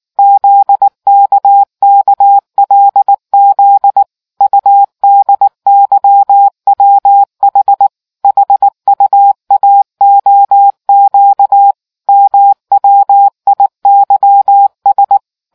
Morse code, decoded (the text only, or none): ZKKLZ UDYWH HUAOQ MWIYS